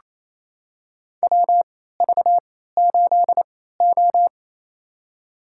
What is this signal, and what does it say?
Morse code, heard as a keyed tone, about 28 wpm, spaced as for 16 wpm, 700 Hz. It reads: WV8O
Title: USA Call Signs - 3x Character Spacing - 28wpm